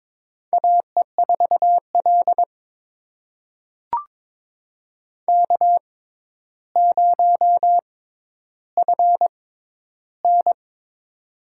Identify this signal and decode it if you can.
Morse code, heard as a keyed tone, about 22 wpm, spaced as for 11 wpm, 700 Hz. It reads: AE4L K0FN
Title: USA Call Signs - 4 Characters - 6x Character Spacing - 22wpm